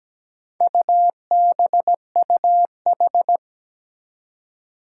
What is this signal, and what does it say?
Morse code, heard as a keyed tone, about 17 wpm, 700 Hz. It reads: UBUH
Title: Four Challenging Letter-Number Combinations 17wpm